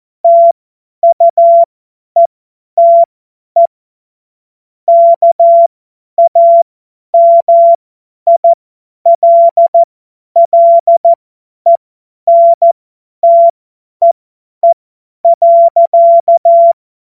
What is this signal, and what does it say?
Morse code, heard as a keyed tone, about 14 wpm, spaced as for 10 wpm, 680 Hz. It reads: TUETE KAMILLENTEE.
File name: Morsekurs / Witze / Witz 30 Januar 2024 50 F70